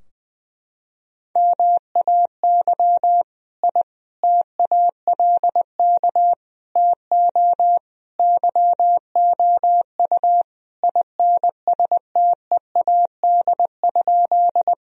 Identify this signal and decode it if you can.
Morse code, heard as a keyed tone, about 21 wpm, 700 Hz. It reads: MAY I TALK TO YOU INSTEAD?
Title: Sentences from Top 1000 Words - Review Part 4 - 20wpm